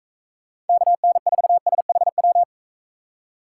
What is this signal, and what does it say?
Morse code, heard as a keyed tone, about 41 wpm, 700 Hz. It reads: KN4SHW